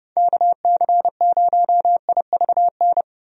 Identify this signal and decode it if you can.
Morse code, heard as a keyed tone, about 31 wpm, 700 Hz. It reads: KC0IVN
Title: USA Call Signs - 6 Characters - Repeated 2x - 30wpm